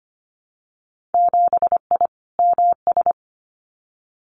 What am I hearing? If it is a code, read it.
Morse code, heard as a keyed tone, about 25 wpm, 700 Hz. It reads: 7I MS
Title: Winter Field Day Exchanges Encore 25wpm